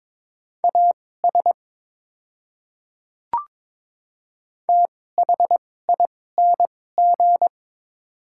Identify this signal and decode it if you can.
Morse code, heard as a keyed tone, about 22 wpm, spaced as for 16 wpm, 690 Hz. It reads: AS THING
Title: 2x Character Spacing - Top 100 Words - 22wpm